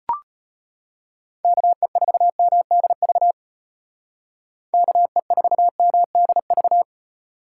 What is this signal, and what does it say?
Morse code, heard as a keyed tone, about 30 wpm, 700 Hz. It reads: KE4MDV KE4MDV
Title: Speed Racing - USA 2x3 Call Signs 25wpm